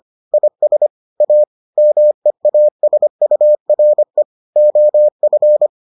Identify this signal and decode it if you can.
Morse code, heard as a keyed tone, about 25 wpm, 600 Hz. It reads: IS A MEASURE OF